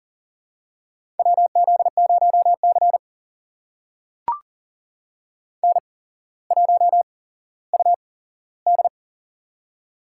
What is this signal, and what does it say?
Morse code, heard as a keyed tone, about 40 wpm, spaced as for 17 wpm, 700 Hz. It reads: WZ0C N1UD
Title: USA Call Signs - 8x Character Spacing - 40wpm